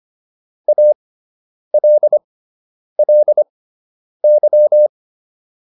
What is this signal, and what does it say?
Morse code, heard as a keyed tone, about 25 wpm, spaced as for 9 wpm, 600 Hz. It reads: ALLY